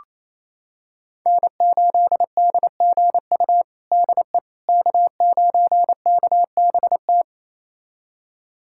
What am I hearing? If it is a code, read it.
Morse code, heard as a keyed tone, about 28 wpm, 700 Hz. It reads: N8DGU DE K9KBT
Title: USA Callsign DE Callsign - 5 Characters - 28wpm